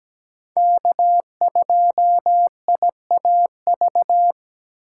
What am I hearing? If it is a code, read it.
Morse code, heard as a keyed tone, about 17 wpm, 700 Hz. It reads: K2IAV
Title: USA Call Signs - 5 Characters - 2x Character Spacing - 17wpm